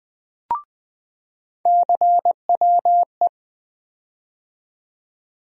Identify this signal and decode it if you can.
Morse code, heard as a keyed tone, about 20 wpm, 700 Hz. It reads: CWE
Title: USA Call Sign Postfixes - 3 Letters 20wpm